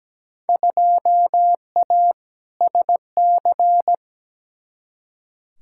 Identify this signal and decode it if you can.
Morse code, heard as a keyed tone, about 17 wpm, 700 Hz. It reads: 2A SC
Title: Field Day Exchanges - 17wpm